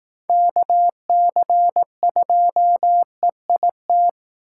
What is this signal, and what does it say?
Morse code, heard as a keyed tone, about 18 wpm, 700 Hz. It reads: KC2EIT
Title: Speed Racing - USA 2x3 Call Signs 15wpm